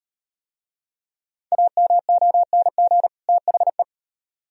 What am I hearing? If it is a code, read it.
Morse code, heard as a keyed tone, about 37 wpm, 700 Hz. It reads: AMONG THE